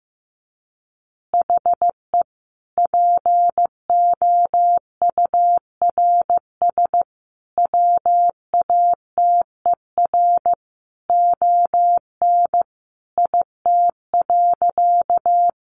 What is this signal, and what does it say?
Morse code, heard as a keyed tone, about 15 wpm, 700 Hz. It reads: HE POURS WATER ON IT.
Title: Additional Easy Phrases 15wpm